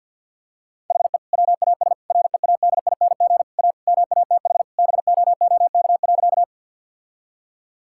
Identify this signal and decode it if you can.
Morse code, heard as a keyed tone, about 50 wpm, 700 Hz. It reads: HE WAS READING A MATH BOOK.
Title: Sentences from Top 1000 Words - Review Part 6 - 50wpm